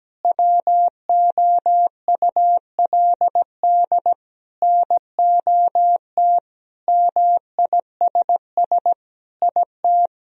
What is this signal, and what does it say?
Morse code, heard as a keyed tone, about 17 wpm, 700 Hz. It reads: WOULD NOT MISS IT
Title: Sentences from Top 1000 Words - Review Part 3 - 17wpm